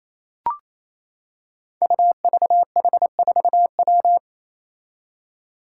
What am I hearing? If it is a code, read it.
Morse code, heard as a keyed tone, about 28 wpm, 700 Hz. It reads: UVH4W
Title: Five Challenging Letter-Number Combinations 28wpm